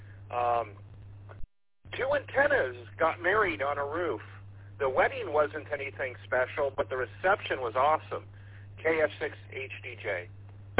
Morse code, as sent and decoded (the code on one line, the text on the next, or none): none